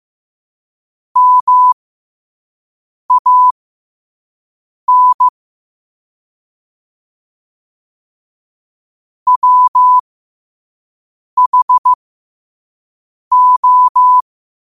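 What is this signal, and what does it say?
Morse code, heard as a keyed tone, about 15 wpm, spaced as for 5 wpm, 1000 Hz. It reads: MAN WHO
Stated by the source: Morse code sent at 5 wpm for learning.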